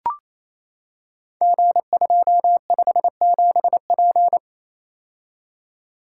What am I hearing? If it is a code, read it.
Morse code, heard as a keyed tone, about 27 wpm, 700 Hz. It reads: G257P
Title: Code Groups - 5 Letter:Numbers - 28wpm